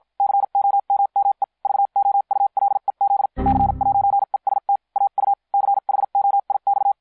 Morse code, heard as a keyed tone, about 50 wpm, spaced as for 43 wpm, 800 Hz. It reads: COMME VOULEZ C'EST AU CHOIX